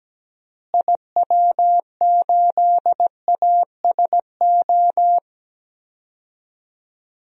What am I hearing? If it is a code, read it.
Morse code, heard as a keyed tone, about 17 wpm, 700 Hz. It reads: IW8ASO
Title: DXCC-100 Call Signs - 6 Characters - Repeated 2x - 17wpm